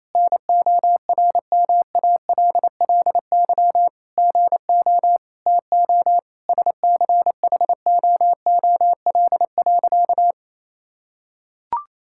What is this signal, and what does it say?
Morse code, heard as a keyed tone, about 27 wpm, 700 Hz. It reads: NORMALLY GO TO SCHOOL.
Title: Sentences from Top 1000 Words - Review Part 6 - 28wpm